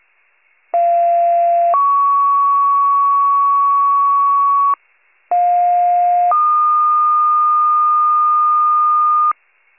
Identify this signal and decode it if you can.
Two-tone page, heard as a one-second tone, then a three-second tone, about 690 then 1120 Hz, about 710 then 1220 Hz. It is heard from 0.7 to 9.3 s.
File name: wcalert